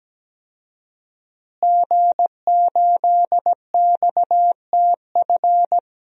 Morse code, heard as a keyed tone, about 17 wpm, 700 Hz. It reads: G8XTF
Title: DXCC-100 Call Signs - 5 Characters - Repeated 3x - 17wpm